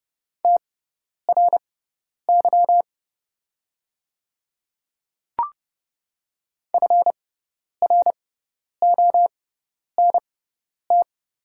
Morse code, heard as a keyed tone, about 30 wpm, spaced as for 10 wpm, 690 Hz. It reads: TRY FRONT
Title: 6x Character Spacing - Top 500 Words - 30wpm